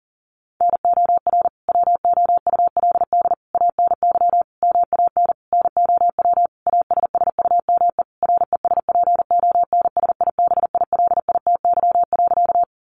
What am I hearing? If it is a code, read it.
Morse code, heard as a keyed tone, about 40 wpm, 700 Hz. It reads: NOR WOULD ANY MAN NOW ASSUME RESPONSIBILITY.
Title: Sentences from Top 1500 Words 40wpm